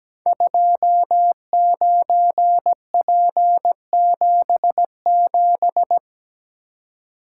Morse code, heard as a keyed tone, about 17 wpm, 700 Hz. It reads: 29P77
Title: Code Groups - 5 Letter:Numbers - 17wpm